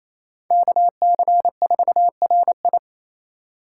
Morse code, heard as a keyed tone, about 37 wpm, 700 Hz. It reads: KC4RI